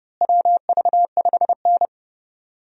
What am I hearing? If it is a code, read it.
Morse code, heard as a keyed tone, about 30 wpm, 700 Hz. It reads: WV5N
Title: USA Call Signs - 4 Characters - Repeated 3x - 30wpm